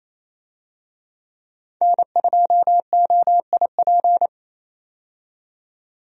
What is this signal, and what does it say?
Morse code, heard as a keyed tone, about 27 wpm, 700 Hz. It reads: N2OIP
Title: USA Call Signs - 5 Characters - 28wpm